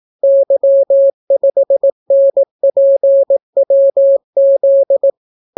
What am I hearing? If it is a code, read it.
Morse code, heard as a keyed tone, about 18 wpm, 550 Hz. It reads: Y5NPWZ